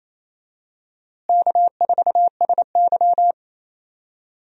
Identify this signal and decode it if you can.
Morse code, heard as a keyed tone, about 27 wpm, 700 Hz. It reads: K4SY
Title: USA Call Signs - 4 Characters - Repeated 2x - 28wpm